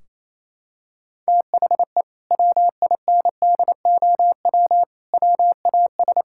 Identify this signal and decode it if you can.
Morse code, heard as a keyed tone, about 28 wpm, 700 Hz. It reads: THE WINDOW WAS
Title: Sentences from Top 1600 Words 28wpm